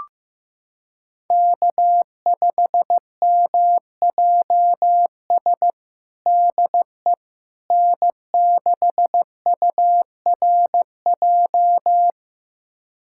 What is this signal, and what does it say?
Morse code, heard as a keyed tone, about 16 wpm, 700 Hz. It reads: K5MJS DE N6URJ
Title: USA Callsign DE Callsign - 5 Characters - 15wpm